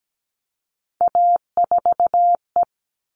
Morse code, heard as a keyed tone, about 17 wpm, 700 Hz. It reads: A4E
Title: All Cut Numbers - 3-Digit 17wpm